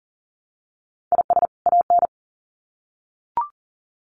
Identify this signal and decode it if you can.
Morse code, heard as a keyed tone, about 40 wpm, 700 Hz. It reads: IS AN
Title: Sets of 2 Words 40wpm